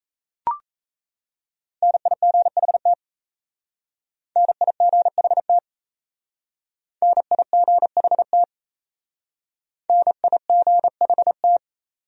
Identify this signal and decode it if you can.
Morse code, heard as a keyed tone, about 37 wpm, 700 Hz. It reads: NIGHT NIGHT NIGHT NIGHT